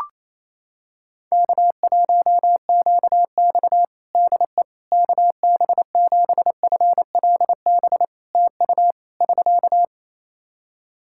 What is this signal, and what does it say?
Morse code, heard as a keyed tone, about 28 wpm, 700 Hz. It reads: K1QX DE KB7FLB TU <SK>
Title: Practice QSO - Exchange 4 - Ending - 28wpm